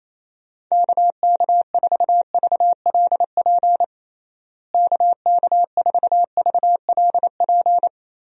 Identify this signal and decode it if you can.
Morse code, heard as a keyed tone, about 28 wpm, 700 Hz. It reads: KK4VLP KK4VLP
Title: USA Call Signs - 6 Characters - Repeated 3x - 28wpm